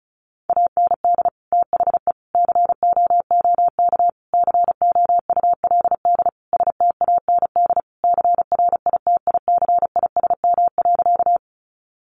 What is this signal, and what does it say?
Morse code, heard as a keyed tone, about 34 wpm, 700 Hz. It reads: AND THE COOK COULD STAND CRITICISM.
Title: Sentences from Top 1300 Words 35wpm